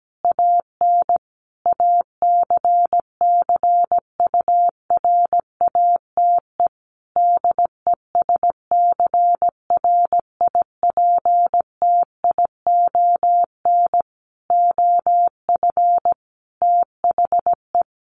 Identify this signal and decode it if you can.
Morse code, heard as a keyed tone, about 17 wpm, 700 Hz. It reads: AN ACCURATE DESCRIPTION OF THE
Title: Sentences from Top 1300 Words 17wpm